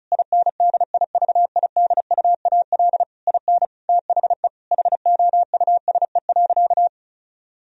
Morse code, heard as a keyed tone, about 34 wpm, 700 Hz. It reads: INDIVIDUAL IN THE HOUSE.
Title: Sentences from Top 1000 Words - Review Part 4 - 35wpm